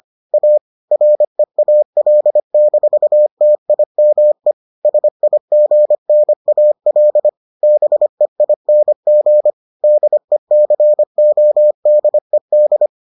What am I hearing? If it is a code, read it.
Morse code, heard as a keyed tone, about 25 wpm, 600 Hz. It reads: A REAL-TIME SIGNAL BEING DECODED